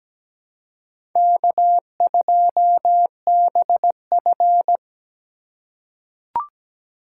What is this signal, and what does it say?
Morse code, heard as a keyed tone, about 17 wpm, 700 Hz. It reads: K2BF
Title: USA Call Signs - 4 Characters - 4x Character Spacing - 17wpm